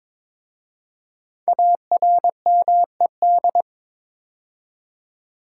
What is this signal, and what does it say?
Morse code, heard as a keyed tone, about 22 wpm, 700 Hz. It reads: ARMED
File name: 5 Letter Words - First 3 Letters Given - 22wpm